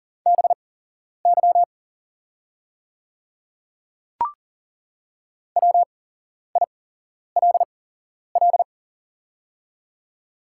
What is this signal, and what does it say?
Morse code, heard as a keyed tone, about 40 wpm, spaced as for 11 wpm, 700 Hz. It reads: BY WILL
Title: 8x Character Spacing - Top 100 Words - 40wpm